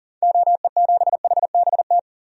Morse code, heard as a keyed tone, about 40 wpm, 700 Hz. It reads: OE7HBT
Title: DXCC-100 Call Signs - 6 Characters - 40wpm